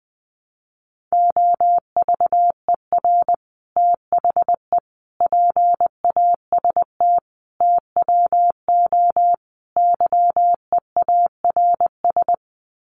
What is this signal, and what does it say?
Morse code, heard as a keyed tone, about 19 wpm, 700 Hz. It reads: OVER THE PAST TWO YEARS
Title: Sets of 5 Words Encore 20wpm